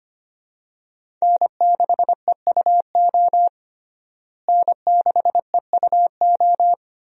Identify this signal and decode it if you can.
Morse code, heard as a keyed tone, about 25 wpm, 700 Hz. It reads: N6EUO N6EUO
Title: USA Call Signs - 5 Characters - Repeated 3x - 25wpm